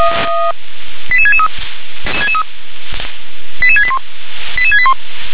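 Five-tone selective call (ZVEI), heard as quick runs of five tones, tones of about 70 ms. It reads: 88663 88681 88681
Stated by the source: ZVEI five-tone call received via SDR.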